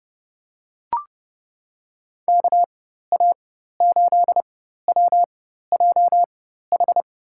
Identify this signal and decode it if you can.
Morse code, heard as a keyed tone, about 30 wpm, spaced as for 14 wpm, 700 Hz. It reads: KA8WJH
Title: USA Call Signs - 6 Characters - 4x Character Spacing - 30wpm